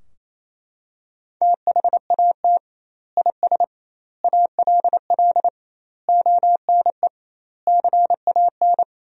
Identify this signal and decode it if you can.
Morse code, heard as a keyed tone, about 28 wpm, 700 Hz. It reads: THAT IS ALL ONE CAN
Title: 2x Word-Spacing - Sentences from Top 200 Words - 28wpm